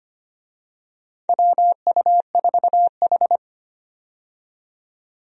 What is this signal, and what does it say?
Morse code, heard as a keyed tone, about 25 wpm, 700 Hz. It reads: WU4H
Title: USA Call Signs - 4 Characters - 25wpm